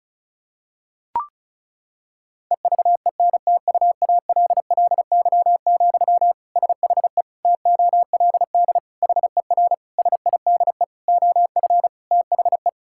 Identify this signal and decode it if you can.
Morse code, heard as a keyed tone, about 34 wpm, 700 Hz. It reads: EVENTUALLY, SHE TOLD HER SIDE OF THE
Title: Sentences from Top 1000 Words - Review Part 6 - 35wpm